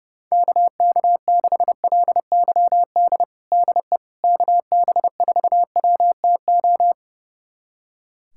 Morse code, heard as a keyed tone, about 30 wpm, 700 Hz. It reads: KK6LYD DE KB4WTO
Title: USA Callsign DE Callsign - 6 Characters - 30wpm